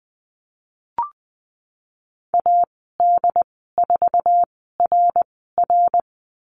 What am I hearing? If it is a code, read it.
Morse code, heard as a keyed tone, about 20 wpm, spaced as for 15 wpm, 700 Hz. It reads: AD4RR
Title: 076 Lesson 13g - Letter L - Call Signs 20wpm